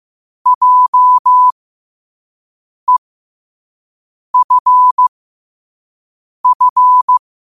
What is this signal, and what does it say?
Morse code, heard as a keyed tone, about 15 wpm, spaced as for 5 wpm, 1000 Hz. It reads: JEFF
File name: qotdfiles / 5wpm